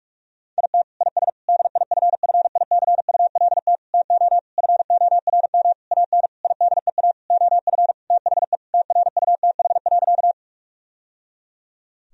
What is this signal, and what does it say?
Morse code, heard as a keyed tone, about 41 wpm, 700 Hz. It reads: IT IS DIFFICULT TO FORM AN IDEA OF THE TRUTH.
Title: Sentences from Top 1000 Words - Review Part 1 - 45wpm